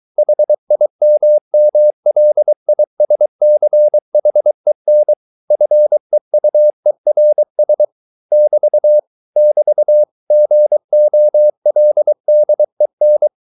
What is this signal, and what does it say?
Morse code, heard as a keyed tone, about 23 wpm, 600 Hz. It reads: HIMMLISCHEN FEUERS <BT> <BT> GOLDEN